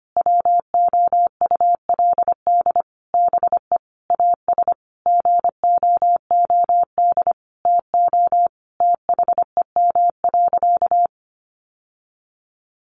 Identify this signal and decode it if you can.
Morse code, heard as a keyed tone, about 25 wpm, 700 Hz. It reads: WOULD BE AS GOOD TO THEM.
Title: Sentences from Top 1200 Words 25wpm